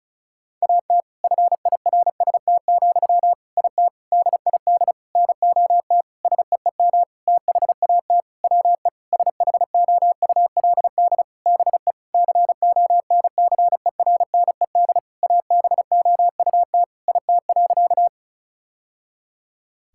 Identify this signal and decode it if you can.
Morse code, heard as a keyed tone, about 36 wpm, 700 Hz. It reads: AT FIRST, IT DID NOT SEEM THAT WE SHOULD BE CONCERNED ABOUT IT.
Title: Sentences from Top 1000 Words - Review Part 6 - 35wpm